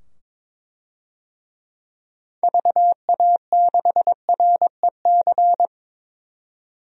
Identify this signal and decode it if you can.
Morse code, heard as a keyed tone, about 21 wpm, 700 Hz. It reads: VA6REC
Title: DXCC-100 Call Signs - 6 Characters - 22wpm